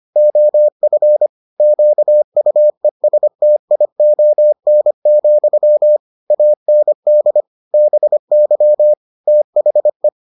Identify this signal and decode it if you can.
Morse code, heard as a keyed tone, about 25 wpm, 600 Hz. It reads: OF QUESTION, AND BY THE